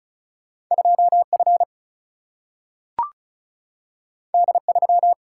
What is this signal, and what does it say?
Morse code, heard as a keyed tone, about 36 wpm, 700 Hz. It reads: 2F D3
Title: Single-Letter Single-Number 35wpm